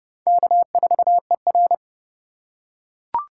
Morse code, heard as a keyed tone, about 30 wpm, 700 Hz. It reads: K4ER
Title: USA Call Signs - 4 Characters - 4x Character Spacing - 30wpm